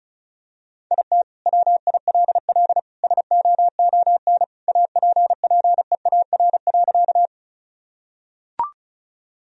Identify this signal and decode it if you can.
Morse code, heard as a keyed tone, about 36 wpm, 700 Hz. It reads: IT WILL SOON APPEAR.
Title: Sentences from Top 1000 Words - Review Part 2 - 35wpm